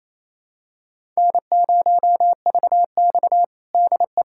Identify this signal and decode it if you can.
Morse code, heard as a keyed tone, about 28 wpm, 700 Hz. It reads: N0VX DE
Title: USA Callsign DE Callsign - 4 Characters - 28wpm